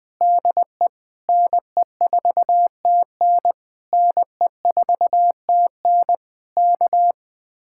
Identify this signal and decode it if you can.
Morse code, heard as a keyed tone, about 20 wpm, 700 Hz. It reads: DE NE4TN NE4TN K
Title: Speed-Racing Common QSO Phrases - 17wpm